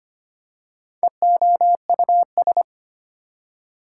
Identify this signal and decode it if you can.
Morse code, heard as a keyed tone, about 25 wpm, 700 Hz. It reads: EOUS
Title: Prefixes and Suffixes 25wpm